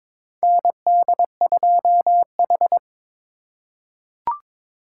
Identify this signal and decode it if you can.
Morse code, heard as a keyed tone, about 22 wpm, 700 Hz. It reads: ND2H
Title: USA Call Signs - 4 Characters - 4x Character Spacing - 22wpm